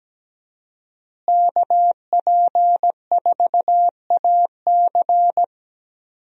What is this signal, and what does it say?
Morse code, heard as a keyed tone, about 17 wpm, 700 Hz. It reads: KP4AC
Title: USA Call Signs - 5 Characters - Repeated 3x - 17wpm